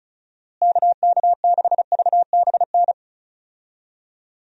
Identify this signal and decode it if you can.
Morse code, heard as a keyed tone, about 36 wpm, 700 Hz. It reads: KK6VBN
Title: USA Call Signs - 6 Characters - Repeated 3x - 35wpm